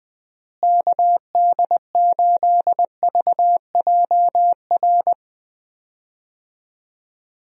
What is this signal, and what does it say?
Morse code, heard as a keyed tone, about 21 wpm, 700 Hz. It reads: KD8VJR
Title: USA Call Signs - 6 Characters - Repeated 2x - 20wpm